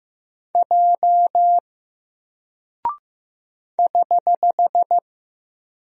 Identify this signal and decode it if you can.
Morse code, heard as a keyed tone, about 15 wpm, 700 Hz. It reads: J <HH>
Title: Single Character (Letter:Number:Punctuation:Prosign) Alt 15wpm